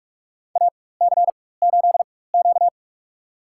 Morse code, heard as a keyed tone, about 46 wpm, spaced as for 21 wpm, 700 Hz. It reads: AC8Q
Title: USA Call Signs - 4x Character Spacing - 45wpm